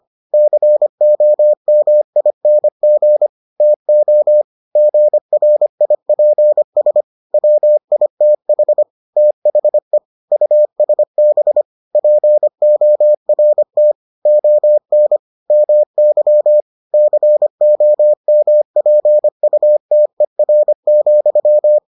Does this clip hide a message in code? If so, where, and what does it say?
Yes: Morse code from 0.3 to 21.8 s, heard as a keyed tone, about 25 wpm, 600 Hz. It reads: COMING TO GRIPS WITH THE USB PORT ON MY COMPUTER,